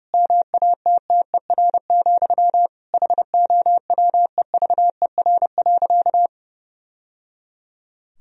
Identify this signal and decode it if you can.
Morse code, heard as a keyed tone, about 30 wpm, 700 Hz. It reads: MATTER, HOWEVER.
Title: Sentences from Top 1000 Words - Review Part 4 - 30wpm